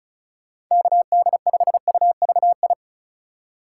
Morse code, heard as a keyed tone, about 36 wpm, 700 Hz. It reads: KD5UVI
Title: USA Call Signs - 6 Characters - 3x Character Spacing - 35wpm